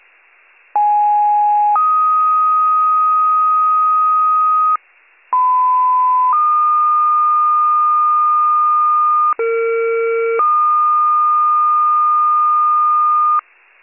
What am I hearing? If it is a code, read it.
Two-tone page, heard as a one-second tone, then a three-second tone, about 830 then 1250 Hz, about 1010 then 1220 Hz, about 460 then 1220 Hz. It is heard from 0.7 to 13.4 s.